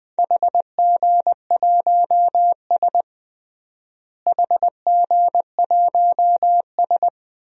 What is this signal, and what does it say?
Morse code, heard as a keyed tone, about 20 wpm, 700 Hz. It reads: HG1S HG1S
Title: DXCC-100 Call Signs - 4 Characters - Repeated 2x - 20wpm